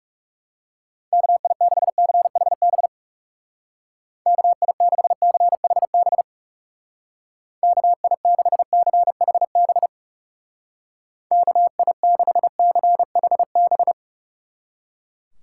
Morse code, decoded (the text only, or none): KI6CHB KI6CHB KI6CHB KI6CHB